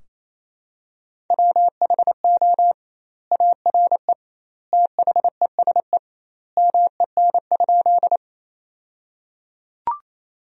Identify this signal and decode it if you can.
Morse code, heard as a keyed tone, about 28 wpm, 700 Hz. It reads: WHO ARE THESE MEN?
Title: 2x Word-Spacing - Sentences from Top 200 Words - 28wpm